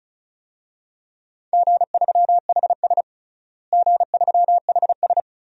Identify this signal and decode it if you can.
Morse code, heard as a keyed tone, about 36 wpm, 700 Hz. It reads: G3HS G3HS
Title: DXCC-100 Call Signs - 4 Characters - Repeated 3x - 35wpm